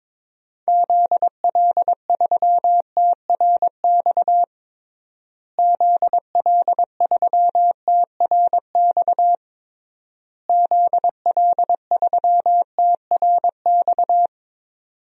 Morse code, heard as a keyed tone, about 22 wpm, 700 Hz. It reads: ZL3TRX ZL3TRX ZL3TRX
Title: DXCC-100 Call Signs - 6 Characters - Repeated 3x - 22wpm